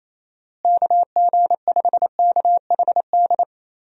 Morse code, heard as a keyed tone, about 28 wpm, 700 Hz. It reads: KG5KHD
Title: USA Call Signs - 6 Characters - Repeated 3x - 28wpm